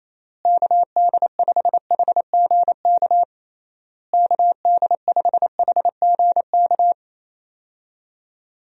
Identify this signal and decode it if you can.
Morse code, heard as a keyed tone, about 28 wpm, 700 Hz. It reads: KD5HGK KD5HGK
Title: USA Call Signs - 6 Characters - Repeated 3x - 28wpm